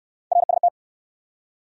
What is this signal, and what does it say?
Morse code, heard as a keyed tone, about 70 wpm, 700 Hz. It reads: RST